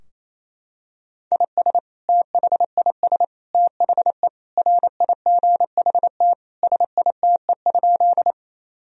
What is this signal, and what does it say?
Morse code, heard as a keyed tone, about 28 wpm, 700 Hz. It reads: IS THIS THE RIGHT SITE?
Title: Sentences from Top 1000 Words - Review Part 4 - 28wpm